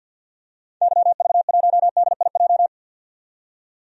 Morse code, heard as a keyed tone, about 50 wpm, 700 Hz. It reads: YV1DIJ
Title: DXCC-100 Call Signs - 6 Characters - 50wpm